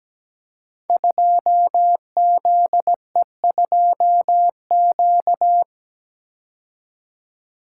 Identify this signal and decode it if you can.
Morse code, heard as a keyed tone, about 17 wpm, 700 Hz. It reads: 2ZE2Q
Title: Code Groups - 5 Letter:Numbers - 17wpm